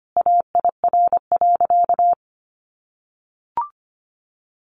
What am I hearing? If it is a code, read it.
Morse code, heard as a keyed tone, about 25 wpm, 700 Hz. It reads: AIR.
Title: Sentences from Top 400 Words 25wpm